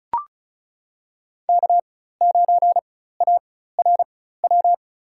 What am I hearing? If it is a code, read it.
Morse code, heard as a keyed tone, about 36 wpm, spaced as for 16 wpm, 700 Hz. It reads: K9ARW
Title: USA Call Signs - 5 Characters - 4x Character Spacing - 35wpm